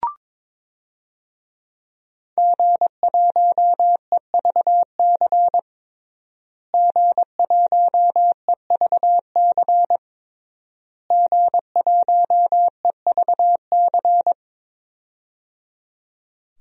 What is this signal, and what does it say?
Morse code, heard as a keyed tone, about 22 wpm, 700 Hz. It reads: G1EVC G1EVC G1EVC
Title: DXCC-100 Call Signs - 5 Characters - Repeated 3x - 22wpm